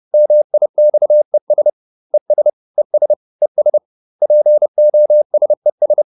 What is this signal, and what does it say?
Morse code, heard as a keyed tone, about 30 wpm, 600 Hz. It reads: MIXES ES ES ES POSES